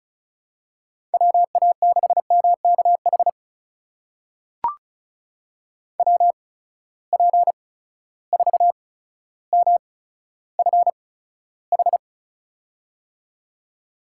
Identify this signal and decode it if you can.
Morse code, heard as a keyed tone, about 34 wpm, spaced as for 15 wpm, 700 Hz. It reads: WA6MKH WP4MFH
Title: USA Call Signs - 8x Character Spacing - 35wpm